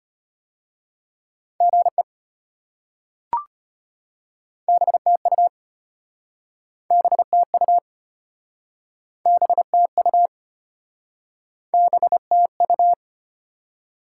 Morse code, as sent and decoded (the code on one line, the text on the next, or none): --. / -... - ..- / -... - ..- / -... - ..- / -... - ..-
GE BTU BTU BTU BTU